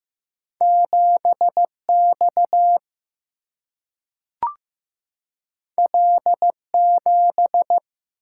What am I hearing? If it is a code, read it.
Morse code, heard as a keyed tone, about 15 wpm, 700 Hz. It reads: 7 X L 7
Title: Single-Letter Single-Number 15wpm